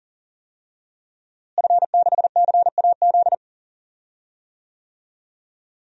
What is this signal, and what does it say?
Morse code, heard as a keyed tone, about 40 wpm, 700 Hz. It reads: F6CAZ
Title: DXCC-100 Call Signs - 5 Characters - Repeated 3x - 40wpm